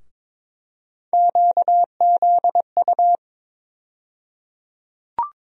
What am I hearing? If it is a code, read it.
Morse code, heard as a keyed tone, about 22 wpm, 700 Hz. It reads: QZU
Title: USA Call Sign Postfixes - 3 Letters 22wpm